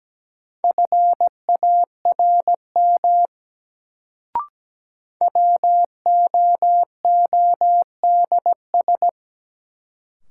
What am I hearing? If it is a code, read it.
Morse code, heard as a keyed tone, about 17 wpm, 700 Hz. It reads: FARM WOODS
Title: Sentences from Top 1000 Words - Review Part 2 - New Word Drill - 17wpm